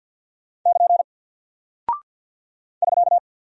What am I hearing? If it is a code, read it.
Morse code, heard as a keyed tone, about 48 wpm, 700 Hz. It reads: <KN> <SK>
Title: Single Character (Letter:Number:Punctuation:Prosign) Alt 50wpm